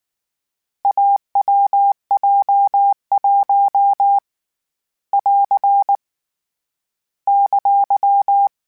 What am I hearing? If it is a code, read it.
Morse code, heard as a keyed tone, about 19 wpm, 800 Hz. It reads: AWJ1 <AR> !
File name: Difficult Combos / Difficult Combos-mp3s / AWJ1 @18